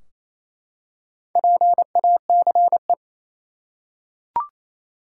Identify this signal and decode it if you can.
Morse code, heard as a keyed tone, about 28 wpm, 700 Hz. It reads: PACE
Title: Four Letter Words - 28wpm